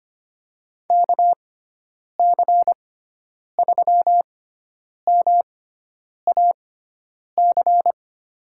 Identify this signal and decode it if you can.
Morse code, heard as a keyed tone, about 25 wpm, spaced as for 9 wpm, 700 Hz. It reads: KC3MAC
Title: USA Call Signs - 6 Characters - 6x Character Spacing - 25wpm